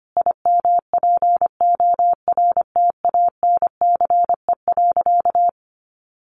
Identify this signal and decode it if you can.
Morse code, heard as a keyed tone, about 25 wpm, 700 Hz. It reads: IMPORTANCE.
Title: Sentences from Top 1500 Words 25wpm